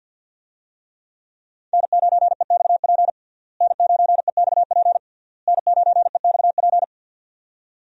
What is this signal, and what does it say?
Morse code, heard as a keyed tone, about 51 wpm, 700 Hz. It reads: N9EXP N9EXP N9EXP
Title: USA Call Signs - 5 Characters - Repeated 3x - 50wpm